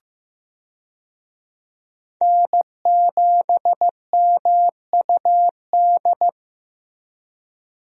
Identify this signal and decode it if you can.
Morse code, heard as a keyed tone, about 16 wpm, 700 Hz. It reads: N7MUD
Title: USA Call Signs - 5 Characters - Repeated 2x - 15wpm